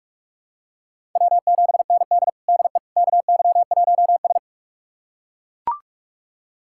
Morse code, heard as a keyed tone, about 41 wpm, 700 Hz. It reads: W7ND DE KY1S
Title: USA Callsign DE Callsign - 4 Characters - 45wpm